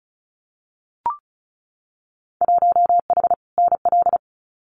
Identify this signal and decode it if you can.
Morse code, heard as a keyed tone, about 36 wpm, 700 Hz. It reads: 1H NL
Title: Winter Field Day Exchanges 35wpm